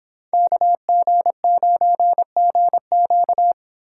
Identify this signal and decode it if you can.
Morse code, heard as a keyed tone, about 25 wpm, 700 Hz. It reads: KG9GQ